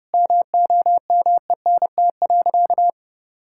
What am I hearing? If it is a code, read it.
Morse code, heard as a keyed tone, about 30 wpm, 700 Hz. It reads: MOMENT.
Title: Sentences from Top 1000 Words - Review Part 5 - 30wpm